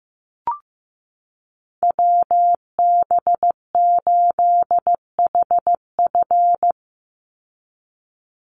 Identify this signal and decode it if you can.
Morse code, heard as a keyed tone, about 15 wpm, 700 Hz. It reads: WB8HF
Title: Call Signs Repeated 3x - 15wpm